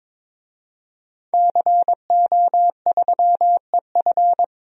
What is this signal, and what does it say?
Morse code, heard as a keyed tone, about 22 wpm, 700 Hz. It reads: CO3EF
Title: DXCC-100 Call Signs - 5 Characters - 22wpm